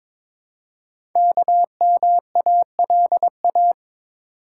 Morse code, heard as a keyed tone, about 22 wpm, 700 Hz. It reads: KMALA
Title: Code Groups - 5 Letters - 22wpm